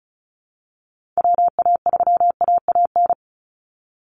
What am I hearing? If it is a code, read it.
Morse code, heard as a keyed tone, about 34 wpm, 700 Hz. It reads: WA3AAN
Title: Call Signs Repeated 2x - 35wpm